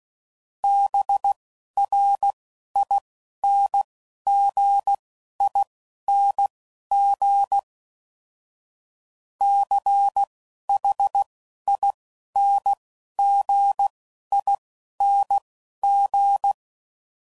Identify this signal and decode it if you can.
Morse code, heard as a keyed tone, about 16 wpm, spaced as for 11 wpm, 800 Hz. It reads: BRINGING CHINGING
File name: Suffixes / Special INGs / Dble INGs